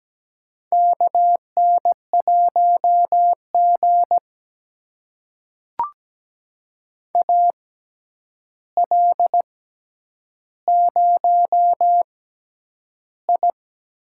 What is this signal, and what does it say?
Morse code, heard as a keyed tone, about 17 wpm, spaced as for 9 wpm, 700 Hz. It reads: KN1G AL0I
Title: USA Call Signs - 4 Characters - 6x Character Spacing - 17wpm